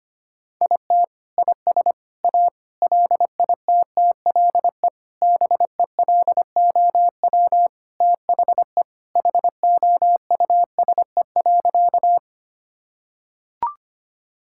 Morse code, heard as a keyed tone, about 25 wpm, 700 Hz. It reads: IT IS A LITTLE BELOW THE HOUSE.
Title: Sentences from Top 1000 Words - Review Part 5 - 25wpm